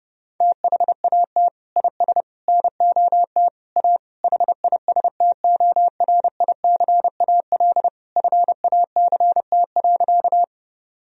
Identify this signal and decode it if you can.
Morse code, heard as a keyed tone, about 32 wpm, 700 Hz. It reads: THAT IS NOT A HISTORICAL FACT.
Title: Sentences from Top 1000 Words - Review Part 4 - 30wpm